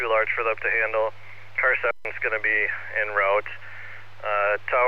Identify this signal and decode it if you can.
Emergency Alert System (SAME) tones, heard as harsh buzzing bursts, with speech. It is heard from 1.2 to 4.1 s.